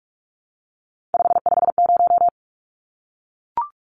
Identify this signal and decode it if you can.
Morse code, heard as a keyed tone, about 44 wpm, 700 Hz. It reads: 550
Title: Numbers - 3-Digit 45wpm